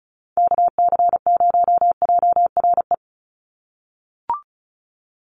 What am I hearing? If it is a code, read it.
Morse code, heard as a keyed tone, about 34 wpm, 700 Hz. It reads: KC0JRE
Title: USA Call Signs 35wpm